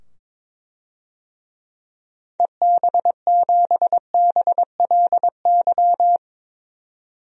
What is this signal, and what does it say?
Morse code, heard as a keyed tone, about 22 wpm, 700 Hz. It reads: EB7BLY
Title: DXCC-100 Call Signs - 6 Characters - 22wpm